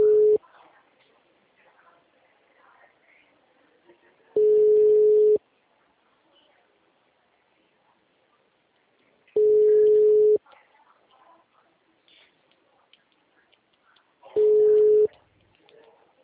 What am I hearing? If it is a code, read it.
Ringback tone (Europe, 425 Hz), with speech.
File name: out- -2012- - - .45427